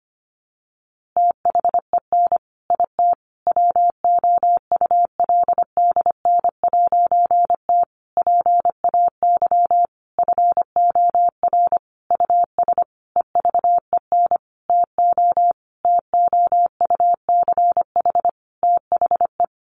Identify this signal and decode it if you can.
Morse code, heard as a keyed tone, about 25 wpm, 700 Hz. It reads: THEN IT WOULDN'T PAY FOR US EVEN TO TOUCH THE
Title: A Sound Of Thunder 25wpm